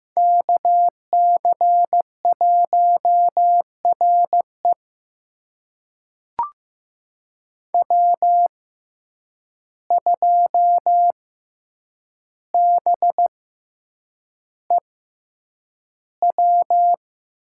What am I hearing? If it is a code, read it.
Morse code, heard as a keyed tone, about 15 wpm, spaced as for 8 wpm, 700 Hz. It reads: KC1RE W2BEW
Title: USA Call Signs - 5 Characters - 6x Character Spacing - 15wpm